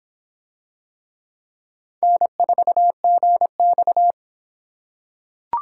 Morse code, heard as a keyed tone, about 25 wpm, 700 Hz. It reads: N4GX